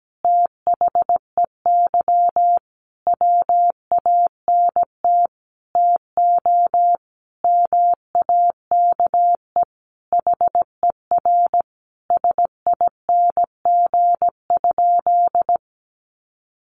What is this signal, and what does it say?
Morse code, heard as a keyed tone, about 17 wpm, 700 Hz. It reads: THEY WANT TO MAKE HER SING?
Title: Sentences from Top 400 Words 17wpm